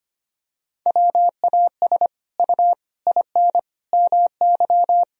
Morse code, heard as a keyed tone, about 25 wpm, 700 Hz. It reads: WAS U IN MY
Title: Sentences from Top 300 Words and Abbreviations 25wpm